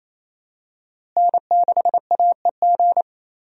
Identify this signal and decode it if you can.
Morse code, heard as a keyed tone, about 28 wpm, 700 Hz. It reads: N6AEG
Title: USA Call Signs - 5 Characters - Repeated 2x - 28wpm